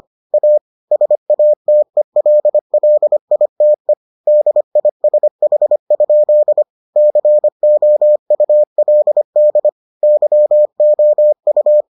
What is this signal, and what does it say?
Morse code, heard as a keyed tone, about 25 wpm, 600 Hz. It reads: A SATELLITE DISH? COULD YOU